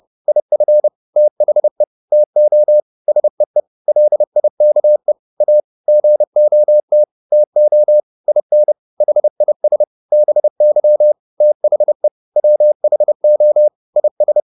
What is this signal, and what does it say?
Morse code, heard as a keyed tone, about 30 wpm, 600 Hz. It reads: IF THE TO SEE LIKE A GOT TO IN HIS BY THE WHO IS